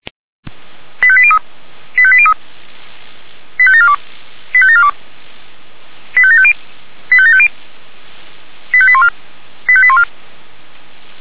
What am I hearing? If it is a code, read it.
Five-tone selective call (ZVEI), heard as quick runs of five tones, tones of about 70 ms. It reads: 85803 85803 85742 85742 85750 85750 85814 85814
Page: ZVEI